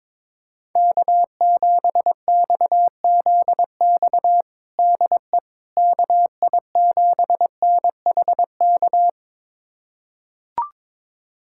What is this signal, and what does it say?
Morse code, heard as a keyed tone, about 21 wpm, 700 Hz. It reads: K7XZX DE KI7NHK
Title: USA Callsign DE Callsign 22wpm